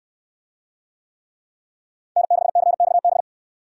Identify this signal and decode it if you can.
Morse code, heard as a keyed tone, about 69 wpm, 700 Hz. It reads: N6CXB